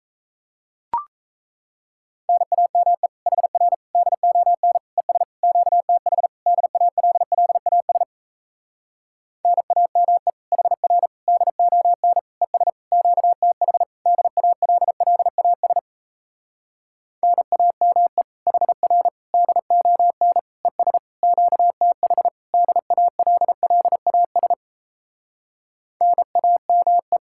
Morse code, heard as a keyed tone, about 37 wpm, 700 Hz. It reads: NAME HR DON ES QTH DALLAS NAME HR DON ES QTH DALLAS NAME HR DON ES QTH DALLAS NAME